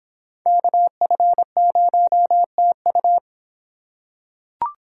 Morse code, heard as a keyed tone, about 24 wpm, 700 Hz. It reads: KF0TU